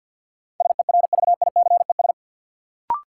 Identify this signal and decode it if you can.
Morse code, heard as a keyed tone, about 52 wpm, 700 Hz. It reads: SERVICES
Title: Sets of 5 Words - New Word Drill - 50wpm